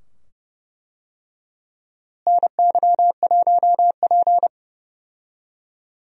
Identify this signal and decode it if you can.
Morse code, heard as a keyed tone, about 30 wpm, 700 Hz. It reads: NY1P
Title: USA Call Signs - 4 Characters - Repeated 2x - 30wpm